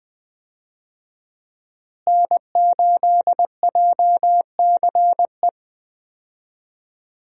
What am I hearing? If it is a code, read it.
Morse code, heard as a keyed tone, about 20 wpm, 700 Hz. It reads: N8JCE
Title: USA Call Signs - 5 Characters - Repeated 3x - 20wpm